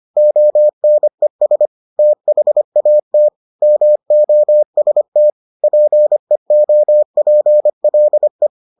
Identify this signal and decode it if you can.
Morse code, heard as a keyed tone, about 25 wpm, 600 Hz. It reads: ONES THAT MOST PEOPLE